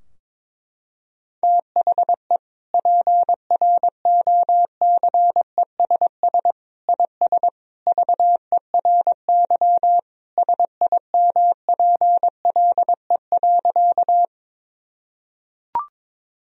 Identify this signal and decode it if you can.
Morse code, heard as a keyed tone, about 22 wpm, 700 Hz. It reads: THE PROCESS IS VERY SIMPLE.
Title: Sentences from Top 1000 Words - Review Part 3 - 22wpm